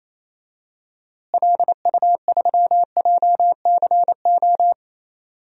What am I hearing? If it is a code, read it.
Morse code, heard as a keyed tone, about 28 wpm, 700 Hz. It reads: LU3JCO
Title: DXCC-100 Call Signs - 6 Characters - Repeated 3x - 28wpm